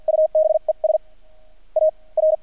Morse code, heard as a keyed tone, about 50 wpm, spaced as for 35 wpm, 630 Hz. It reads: UBES AR
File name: archived / CRSnet